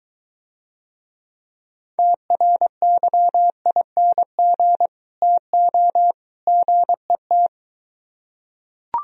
Morse code, heard as a keyed tone, about 23 wpm, 700 Hz. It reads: TRYING TO GET